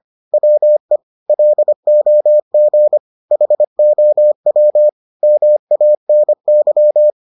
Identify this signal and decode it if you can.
Morse code, heard as a keyed tone, about 25 wpm, 600 Hz. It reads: WE LOG HOW MANY